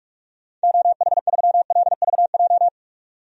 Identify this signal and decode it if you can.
Morse code, heard as a keyed tone, about 46 wpm, 700 Hz. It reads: OH3LVJ
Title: DXCC-100 Call Signs - 6 Characters - 45wpm